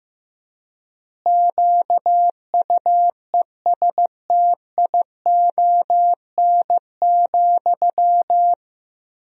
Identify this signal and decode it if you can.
Morse code, heard as a keyed tone, about 15 wpm, 700 Hz. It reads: QUESTION,
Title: 3x Word-Spacing - Sentences from Top 400 Words - 15wpm